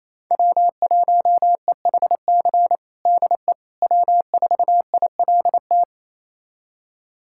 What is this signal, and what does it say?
Morse code, heard as a keyed tone, about 28 wpm, 700 Hz. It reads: W1EHC DE W4ILT
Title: USA Callsign DE Callsign - 5 Characters - 28wpm